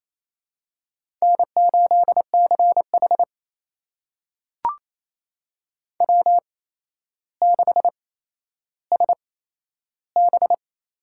Morse code, heard as a keyed tone, about 28 wpm, spaced as for 12 wpm, 700 Hz. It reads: N8CH W6SB
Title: USA Call Signs - 4 Characters - 8x Character Spacing - 28wpm